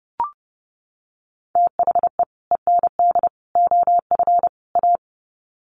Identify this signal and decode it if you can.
Morse code, heard as a keyed tone, about 30 wpm, 700 Hz. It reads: THE END OF A
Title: Sets of 4 Words 30wpm